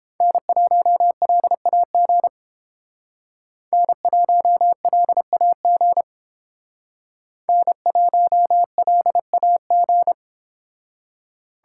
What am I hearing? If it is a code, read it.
Morse code, heard as a keyed tone, about 28 wpm, 700 Hz. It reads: N1LAG N1LAG N1LAG